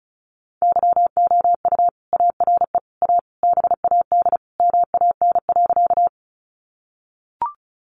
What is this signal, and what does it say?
Morse code, heard as a keyed tone, about 34 wpm, 700 Hz. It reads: YOU ARE A BAD MAN.
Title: Sentences from Top 700 Words 35wpm